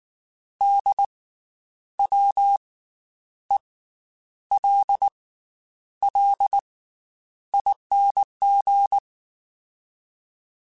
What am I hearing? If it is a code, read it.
Morse code, heard as a keyed tone, about 19 wpm, 800 Hz. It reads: D W E L L ING
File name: Special INGs / Special INGs-mp3s / Slow ing @18